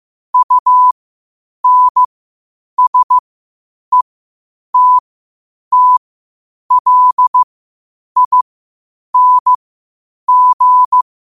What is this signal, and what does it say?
Morse code, heard as a keyed tone, about 15 wpm, spaced as for 8 wpm, 1000 Hz. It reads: UNSETTLING
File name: qotdfiles / 8wpm